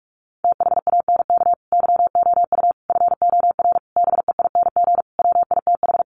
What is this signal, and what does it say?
Morse code, heard as a keyed tone, about 44 wpm, 700 Hz. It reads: THANK YOU FOR BEING WITH
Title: Sets of 5 Words 45wpm